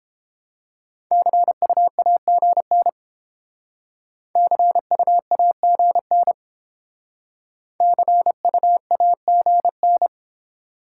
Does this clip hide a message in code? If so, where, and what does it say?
Yes: Morse code from 1.1 to 10.1 s, heard as a keyed tone, about 28 wpm, 690 Hz. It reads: CUAGN CUAGN CUAGN